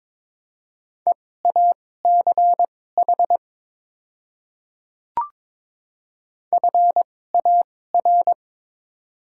Morse code, heard as a keyed tone, about 22 wpm, spaced as for 16 wpm, 700 Hz. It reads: EACH FAR
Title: 2x Character Spacing - Top 300 Words - 22wpm